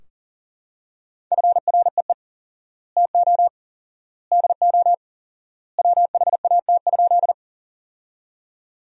Morse code, heard as a keyed tone, about 40 wpm, 700 Hz. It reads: FREE TO DO WHAT?
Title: 4x Word-Spacing - Sentences from Top 500 Words - 40wpm